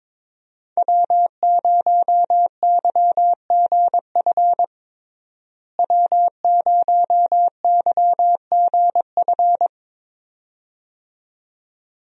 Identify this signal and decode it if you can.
Morse code, heard as a keyed tone, about 21 wpm, 700 Hz. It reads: W0YGF W0YGF
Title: USA Call Signs - 5 Characters - Repeated 2x - 22wpm